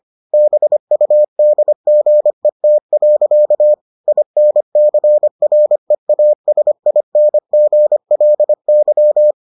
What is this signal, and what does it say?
Morse code, heard as a keyed tone, about 25 wpm, 600 Hz. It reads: BUDGET. INCREASINGLY